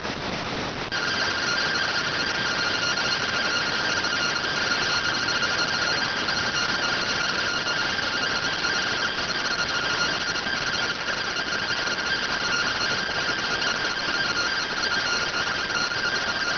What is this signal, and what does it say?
Two-tone teleprinter signal, about 170 Hz shift.